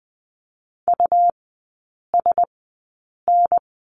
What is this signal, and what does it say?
Morse code, heard as a keyed tone, about 20 wpm, spaced as for 8 wpm, 700 Hz. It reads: USN
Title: 100 Lesson 17e - Letter u - Three Characters 20wpm